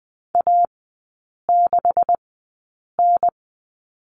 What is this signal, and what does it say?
Morse code, heard as a keyed tone, about 20 wpm, spaced as for 8 wpm, 690 Hz. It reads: A6N
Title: 145 Lesson 24e - Letter y - Three Characters 20wpm